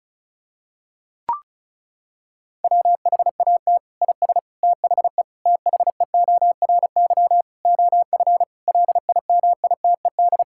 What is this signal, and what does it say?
Morse code, heard as a keyed tone, about 34 wpm, 700 Hz. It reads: WHAT IS THE THEORY OF LIMITED
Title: Sentences from Top 1000 Words - Review Part 6 - 35wpm